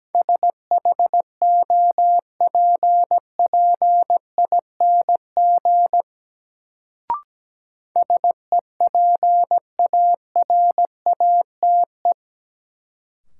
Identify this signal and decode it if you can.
Morse code, heard as a keyed tone, about 17 wpm, 700 Hz. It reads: SHOPPING SEPARATE
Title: Sentences from Top 1000 Words - Review Part 7 - New Word Drill - 17wpm